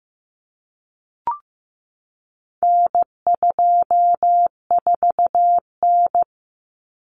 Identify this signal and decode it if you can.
Morse code, heard as a keyed tone, about 15 wpm, 700 Hz. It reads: N24N
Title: Common Cut Numbers - 4-Digit 15wpm